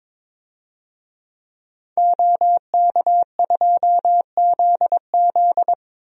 Morse code, heard as a keyed tone, about 22 wpm, 700 Hz. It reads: OK2ZZ
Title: DXCC-100 Call Signs - 5 Characters - 22wpm